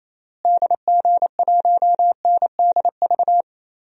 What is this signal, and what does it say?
Morse code, heard as a keyed tone, about 28 wpm, 700 Hz. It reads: DG1NDV
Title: DXCC-100 Call Signs - 6 Characters - Repeated 3x - 28wpm